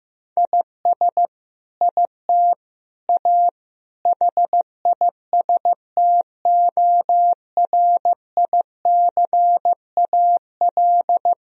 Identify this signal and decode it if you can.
Morse code, heard as a keyed tone, about 16 wpm, 700 Hz. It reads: IS IT A HISTORICAL
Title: Sentences from Top 1000 Words - Review Part 4 - 15wpm